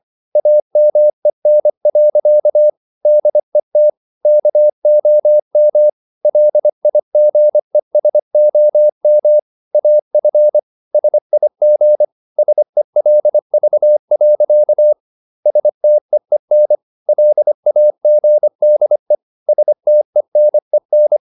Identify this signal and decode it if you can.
Morse code, heard as a keyed tone, about 25 wpm, 600 Hz. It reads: AMEN. DET KOM LIGESOM AF SIG SELV. STEEN LAGDE STENEN
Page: Ferie 24 wpm